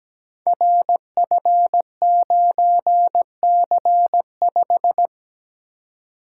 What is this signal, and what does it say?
Morse code, heard as a keyed tone, about 17 wpm, 700 Hz. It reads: RF9C5
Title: Code Groups - 5 Letter:Numbers - 17wpm